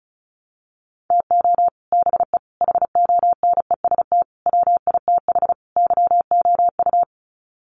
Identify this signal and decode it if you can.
Morse code, heard as a keyed tone, about 34 wpm, 700 Hz. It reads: TO BE HONEST WITH YOU
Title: Sets of 5 Words Encore 35wpm